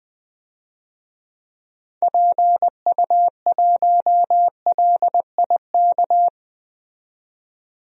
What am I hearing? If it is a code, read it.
Morse code, heard as a keyed tone, about 20 wpm, 700 Hz. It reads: PU1LIK
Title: DXCC-100 Call Signs - 6 Characters - 20wpm